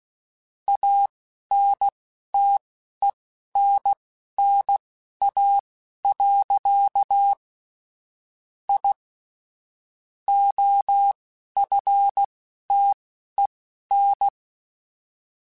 Morse code, heard as a keyed tone, about 16 wpm, spaced as for 11 wpm, 800 Hz. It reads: ANTENNA. I OFTEN